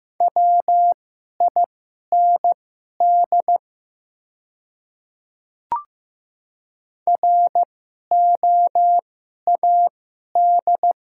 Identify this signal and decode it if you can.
Morse code, heard as a keyed tone, about 15 wpm, spaced as for 11 wpm, 700 Hz. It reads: WIND ROAD
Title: 2x Character Spacing - Top 500 Words - 15wpm